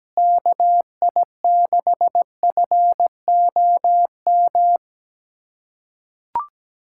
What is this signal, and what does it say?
Morse code, heard as a keyed tone, about 17 wpm, 700 Hz. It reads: KI6FOM
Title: USA Callsign DE Callsign - 6 Characters - 17wpm